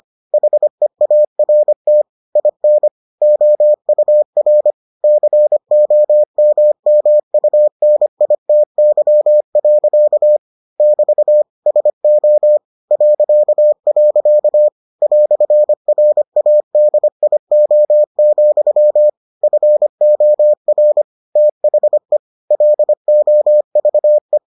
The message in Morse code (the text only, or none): HEART IN OUR COMMUNITY. <BT> SO .. "RADIO, FOR THE LOVE